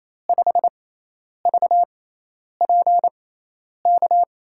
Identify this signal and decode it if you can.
Morse code, heard as a keyed tone, about 28 wpm, spaced as for 10 wpm, 700 Hz. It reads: 5VPK
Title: USA Call Signs - 6x Character Spacing - 28wpm